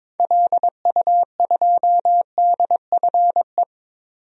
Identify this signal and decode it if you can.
Morse code, heard as a keyed tone, about 22 wpm, 700 Hz. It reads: LU2DFE
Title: DXCC-100 Call Signs - 6 Characters - 22wpm